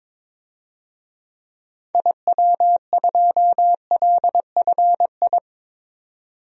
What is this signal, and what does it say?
Morse code, heard as a keyed tone, about 21 wpm, 700 Hz. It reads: IW2LFI
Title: DXCC-100 Call Signs - 6 Characters - 22wpm